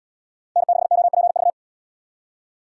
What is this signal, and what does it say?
Morse code, heard as a keyed tone, about 68 wpm, 700 Hz. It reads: N6OPV